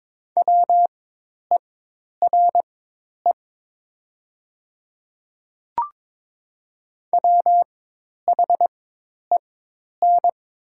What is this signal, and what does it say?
Morse code, heard as a keyed tone, about 22 wpm, spaced as for 10 wpm, 700 Hz. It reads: WERE WHEN
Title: 4x Character Spacing - Top 500 Words - 22wpm